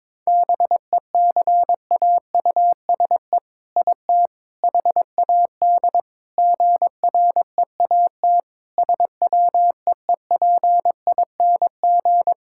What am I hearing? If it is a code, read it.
Morse code, heard as a keyed tone, about 22 wpm, 700 Hz. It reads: BECAUSE IT HAD GREAT SWEEPING
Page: morse-code-ninja-practice-22wpm